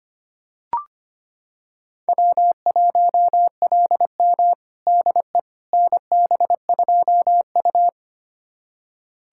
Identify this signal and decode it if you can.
Morse code, heard as a keyed tone, about 25 wpm, 700 Hz. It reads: W1LM DE NB2U
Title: USA Callsign DE Callsign - 4 Characters - 25wpm